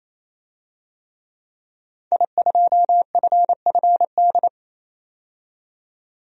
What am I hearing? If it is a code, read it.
Morse code, heard as a keyed tone, about 28 wpm, 700 Hz. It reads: I2FFD
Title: DXCC-100 Call Signs - 5 Characters - 28wpm